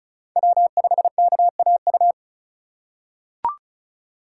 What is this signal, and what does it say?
Morse code, heard as a keyed tone, about 36 wpm, 700 Hz. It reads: W5KAU
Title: USA Call Signs - 2x Character Spacing - 35wpm